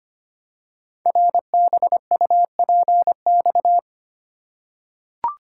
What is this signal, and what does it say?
Morse code, heard as a keyed tone, about 25 wpm, 700 Hz. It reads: RBUPX
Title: Code Groups - 5 Letters - 25wpm